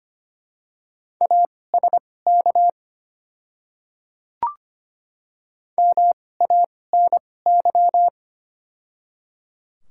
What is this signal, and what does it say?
Morse code, heard as a keyed tone, about 25 wpm, spaced as for 18 wpm, 700 Hz. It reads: ASK MANY
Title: 2x Character Spacing - Top 200 Words - 25wpm